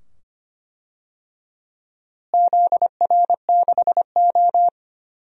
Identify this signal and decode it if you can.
Morse code, heard as a keyed tone, about 25 wpm, 700 Hz. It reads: ZR6O
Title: DXCC-100 Call Signs - 4 Characters - Repeated 2x - 25wpm